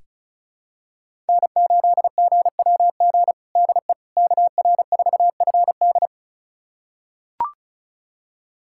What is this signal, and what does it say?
Morse code, heard as a keyed tone, about 34 wpm, 700 Hz. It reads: N8GWG DE KR4FD
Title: USA Callsign DE Callsign - 5 Characters - 35wpm